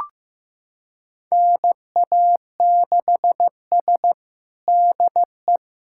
Morse code, heard as a keyed tone, about 15 wpm, 700 Hz. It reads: NA6S DE
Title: USA Callsign DE Callsign - 4 Characters - 15wpm